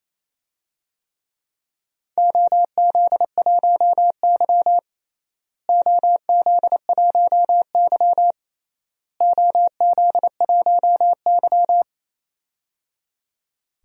Morse code, heard as a keyed tone, about 27 wpm, 700 Hz. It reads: OZ1Y OZ1Y OZ1Y
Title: DXCC-100 Call Signs - 4 Characters - Repeated 3x - 28wpm